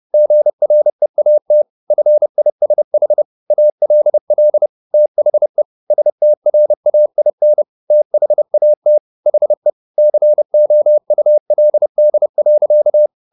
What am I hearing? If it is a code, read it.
Morse code, heard as a keyed tone, about 30 wpm, 600 Hz. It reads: GREAT FISH ALL THE STRAIN THAT HE COULD.